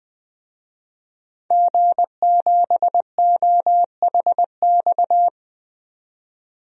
Morse code, heard as a keyed tone, about 20 wpm, 700 Hz. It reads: G7OHX
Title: DXCC-100 Call Signs - 5 Characters - 20wpm